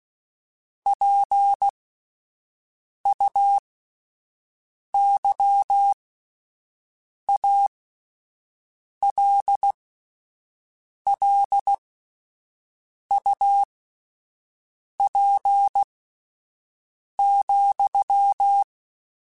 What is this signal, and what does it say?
Morse code, heard as a keyed tone, about 16 wpm, spaced as for 5 wpm, 800 Hz. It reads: PUYALLUP,
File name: QSOs Spaced